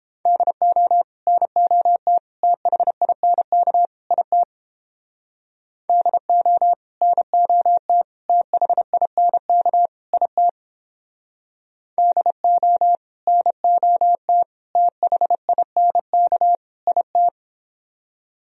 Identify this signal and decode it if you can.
Morse code, heard as a keyed tone, about 29 wpm, 700 Hz. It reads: DO NOT THINK IT DO NOT THINK IT DO NOT THINK IT